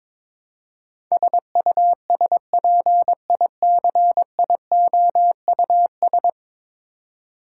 Morse code, heard as a keyed tone, about 22 wpm, 700 Hz. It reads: SUSPICIOUS
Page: morse-code-ninja-practice-22wpm